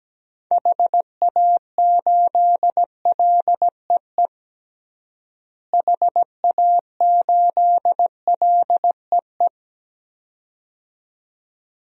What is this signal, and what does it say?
Morse code, heard as a keyed tone, about 17 wpm, 700 Hz. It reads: HA8LEE HA8LEE
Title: DXCC-100 Call Signs - 6 Characters - Repeated 2x - 17wpm